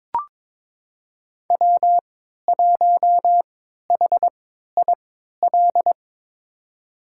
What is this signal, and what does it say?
Morse code, heard as a keyed tone, about 22 wpm, spaced as for 12 wpm, 700 Hz. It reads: W1HIL
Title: USA Call Signs - 3x Character Spacing - 22wpm